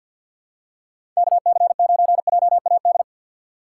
Morse code, heard as a keyed tone, about 53 wpm, 700 Hz. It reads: KC9JAD
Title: USA Call Signs - 6 Characters - 4x Character Spacing - 50wpm